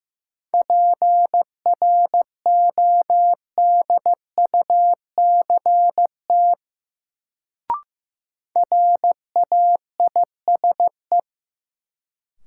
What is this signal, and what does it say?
Morse code, heard as a keyed tone, about 15 wpm, 700 Hz. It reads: PRODUCT RAISE
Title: Sentences from Top 1400 Words - New Word Drill Alt - 15wpm